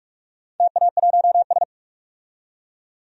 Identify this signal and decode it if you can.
Morse code, heard as a keyed tone, about 44 wpm, 700 Hz. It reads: TA1S
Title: DXCC-100 Call Signs - 4 Characters - 45wpm